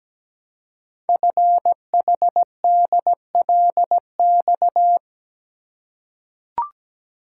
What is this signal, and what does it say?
Morse code, heard as a keyed tone, about 17 wpm, 700 Hz. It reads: FHDLX
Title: Code Groups - 5 Letter:Numbers - 17wpm